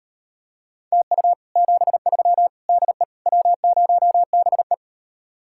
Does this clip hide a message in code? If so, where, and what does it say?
Yes: Morse code from 0.9 to 4.7 s, heard as a keyed tone, about 38 wpm, 700 Hz. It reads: TU 73 DE W0BE